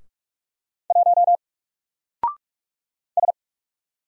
Morse code, heard as a keyed tone, about 44 wpm, 700 Hz. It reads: 1 S
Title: Single Character (Letter:Number:Punctuation:Prosign) 45wpm